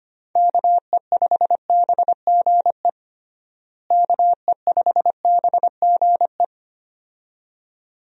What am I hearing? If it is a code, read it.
Morse code, heard as a keyed tone, about 25 wpm, 700 Hz. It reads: KE5BGE KE5BGE
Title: USA Call Signs - 6 Characters - Repeated 2x - 25wpm